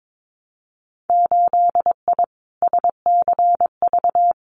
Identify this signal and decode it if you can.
Morse code, heard as a keyed tone, about 22 wpm, 700 Hz. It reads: 8I SCV
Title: Winter Field Day Exchanges Encore 22wpm